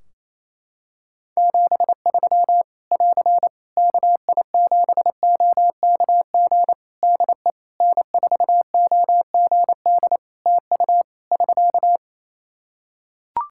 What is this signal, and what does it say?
Morse code, heard as a keyed tone, about 28 wpm, 700 Hz. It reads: 73 <AR> KI7OKG DE N4OGD TU <SK>
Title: K1USN SST QSO Practice 28wpm